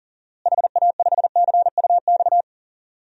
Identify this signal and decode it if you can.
Morse code, heard as a keyed tone, about 40 wpm, 700 Hz. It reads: HA5CUX
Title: DXCC-100 Call Signs - 6 Characters - Repeated 2x - 40wpm